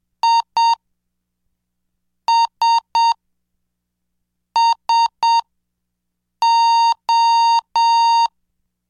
Morse code, heard as a keyed tone, about 7 wpm, spaced as for 4 wpm, 920 Hz. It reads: ISSO